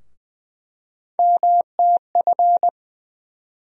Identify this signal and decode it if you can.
Morse code, heard as a keyed tone, about 20 wpm, 700 Hz. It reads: MTF
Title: USA Call Sign Postfixes - 3 Letters 20wpm